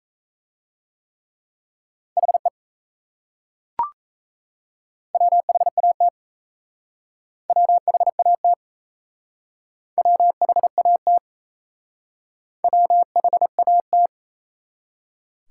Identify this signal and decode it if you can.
Morse code, heard as a keyed tone, about 37 wpm, 700 Hz. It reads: HE WHAT WHAT WHAT WHAT